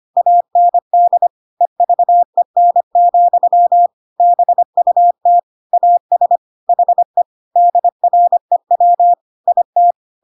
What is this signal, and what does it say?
Morse code, heard as a keyed tone, about 25 wpm, 700 Hz. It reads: AND EVEN, BUT AS HE DREW IT
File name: Pinocchio / Pinocchio ch1 25 25 ik1hge